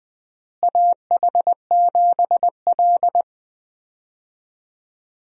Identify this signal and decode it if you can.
Morse code, heard as a keyed tone, about 21 wpm, 700 Hz. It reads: AH7L
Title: USA Call Signs - 4 Characters - Repeated 3x - 20wpm